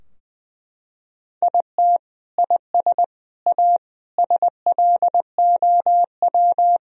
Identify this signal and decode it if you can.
Morse code, heard as a keyed tone, about 20 wpm, 700 Hz. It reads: IT IS A SLOW
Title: Sentences from Top 1000 Words - Review Part 2 - 20wpm